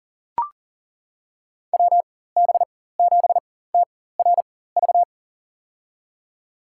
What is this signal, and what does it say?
Morse code, heard as a keyed tone, about 37 wpm, spaced as for 18 wpm, 700 Hz. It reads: WB7TRV
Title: USA Call Signs - 6 Characters - 4x Character Spacing - 40wpm